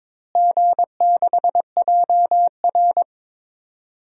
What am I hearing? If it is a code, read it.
Morse code, heard as a keyed tone, about 22 wpm, 700 Hz. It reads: G6JR
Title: DXCC-100 Call Signs - 4 Characters - Repeated 3x - 22wpm